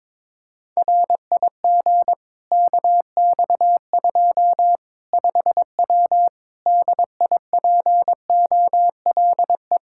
Morse code, heard as a keyed tone, about 22 wpm, 700 Hz. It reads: RIG KX2 5W DIPOLE
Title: Practice QSO - Exchange 2 - Info - 22wpm